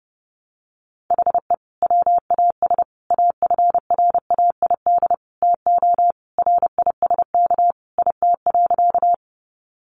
Morse code, heard as a keyed tone, about 30 wpm, 700 Hz. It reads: HE WAS AFRAID TO RISK IT.
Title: Sentences from Top 1500 Words 30wpm